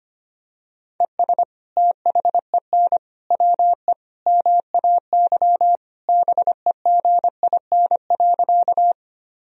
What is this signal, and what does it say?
Morse code, heard as a keyed tone, about 25 wpm, 700 Hz. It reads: ES THEN WE MAY BEGIN.
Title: Sentences from Top 300 Words and Abbreviations 25wpm